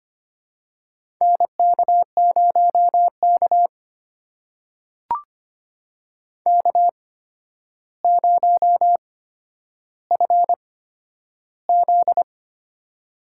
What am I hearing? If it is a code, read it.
Morse code, heard as a keyed tone, about 25 wpm, spaced as for 11 wpm, 700 Hz. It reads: NK0K K0FZ
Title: USA Call Signs - 8x Character Spacing - 25wpm